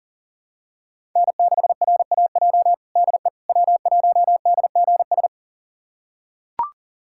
Morse code, heard as a keyed tone, about 40 wpm, 700 Hz. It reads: N6RAJ DE W1DGS
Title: USA Callsign DE Callsign - 5 Characters - 40wpm